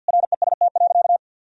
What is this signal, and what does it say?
Morse code, heard as a keyed tone, about 50 wpm, 700 Hz. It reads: REST.
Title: Sentences from Top 1000 Words - Review Part 4 - 50wpm